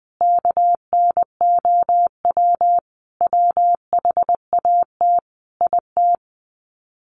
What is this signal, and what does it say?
Morse code, heard as a keyed tone, about 20 wpm, 700 Hz. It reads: KNOW WHAT IT
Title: Sets of 5 Words 20wpm